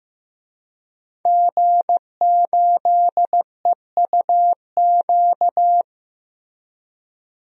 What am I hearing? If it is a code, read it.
Morse code, heard as a keyed tone, about 16 wpm, 700 Hz. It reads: G8EUQ
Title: DXCC-100 Call Signs - 5 Characters - Repeated 3x - 15wpm